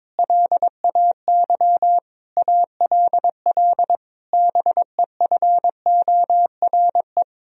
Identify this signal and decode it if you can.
Morse code, heard as a keyed tone, about 21 wpm, 700 Hz. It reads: LAY ALL BEFORE
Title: Sentences from Top 1000 Words - Review Part 2 - 22wpm